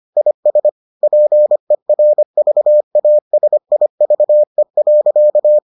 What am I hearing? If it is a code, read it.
Morse code, heard as a keyed tone, about 25 wpm, 600 Hz. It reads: IS PERVASIVE.